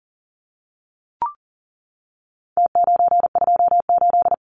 Morse code, heard as a keyed tone, about 40 wpm, 700 Hz. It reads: T928
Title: Numbers - Contest Style - 4-Digit 40wpm